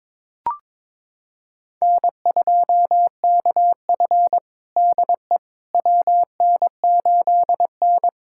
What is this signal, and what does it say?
Morse code, heard as a keyed tone, about 22 wpm, 700 Hz. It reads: N2KF DE WN8N
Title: USA Callsign DE Callsign - 4 Characters - 22wpm